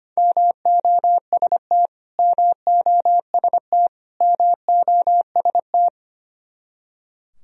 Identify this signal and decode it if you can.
Morse code, heard as a keyed tone, about 25 wpm, 700 Hz. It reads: MOST MOST MOST
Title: Top 100 Words Repeated 25wpm